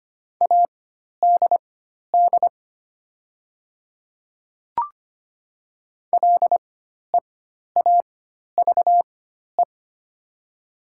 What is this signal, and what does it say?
Morse code, heard as a keyed tone, about 25 wpm, spaced as for 12 wpm, 700 Hz. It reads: ADD LEAVE
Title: 4x Character Spacing - Top 500 Words - 25wpm